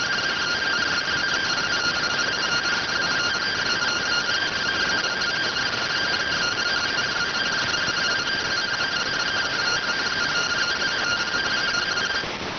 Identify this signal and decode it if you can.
Two-tone teleprinter signal, about 170 Hz shift.